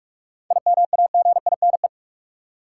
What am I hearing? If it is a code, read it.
Morse code, heard as a keyed tone, about 42 wpm, 700 Hz. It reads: IMAGINE
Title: Sentences from Top 1400 Words - New Word Drill - 45wpm